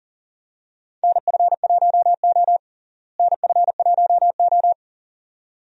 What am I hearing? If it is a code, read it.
Morse code, heard as a keyed tone, about 40 wpm, 700 Hz. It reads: NF1O NF1O
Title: USA Call Signs - 4 Characters - Repeated 2x - 40wpm